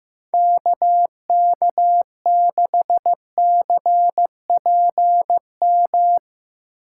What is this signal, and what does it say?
Morse code, heard as a keyed tone, about 15 wpm, 700 Hz. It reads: KK6CPM
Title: USA Call Signs - 6 Characters - 4x Character Spacing - 15wpm